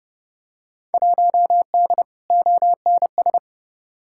Morse code, heard as a keyed tone, about 30 wpm, 700 Hz. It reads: 1D ONS
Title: Field Day Exchanges - 30wpm